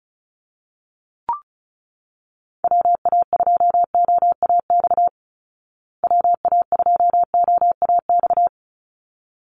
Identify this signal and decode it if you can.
Morse code, heard as a keyed tone, about 34 wpm, 700 Hz. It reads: WA2OAX WA2OAX
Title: Call Signs Repeated 3x - 35wpm